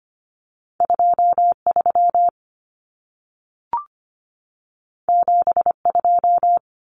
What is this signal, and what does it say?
Morse code, heard as a keyed tone, about 25 wpm, 700 Hz. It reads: 23 72
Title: Common Cut Numbers - 2-Digit 25wpm